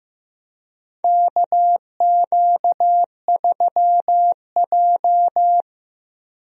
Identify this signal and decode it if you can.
Morse code, heard as a keyed tone, about 15 wpm, 700 Hz. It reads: KQ3J
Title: USA Call Signs - 4 Characters - Repeated 3x - 15wpm